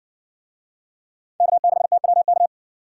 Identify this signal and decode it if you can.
Morse code, heard as a keyed tone, about 60 wpm, 700 Hz. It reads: K6TWK